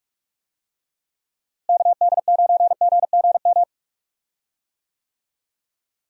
Morse code, heard as a keyed tone, about 46 wpm, 700 Hz. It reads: KD9GGM
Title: USA Call Signs - 6 Characters - Repeated 3x - 45wpm